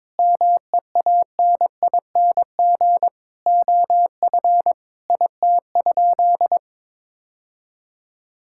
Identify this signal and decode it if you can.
Morse code, heard as a keyed tone, about 22 wpm, 700 Hz. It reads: MEANING OF IT?
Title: Sentences from Top 1000 Words - Review Part 6 - 22wpm